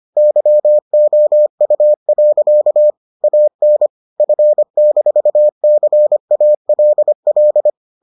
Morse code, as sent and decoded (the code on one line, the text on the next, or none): -.-- --- ..- .-.-.- / .- -. / ..-. -....- -.-. .- .-.. .-..
YOU. AN F-CALL